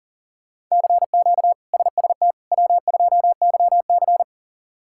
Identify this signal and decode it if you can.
Morse code, heard as a keyed tone, about 40 wpm, 700 Hz. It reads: CQ SST W2YC
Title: K1USN SST QSO Practice - 40wpm